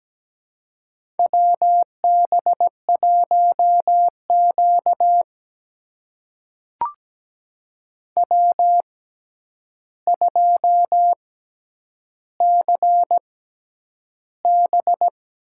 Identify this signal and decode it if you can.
Morse code, heard as a keyed tone, about 17 wpm, spaced as for 9 wpm, 700 Hz. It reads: WB1Q W2CB
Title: USA Call Signs - 4 Characters - 6x Character Spacing - 17wpm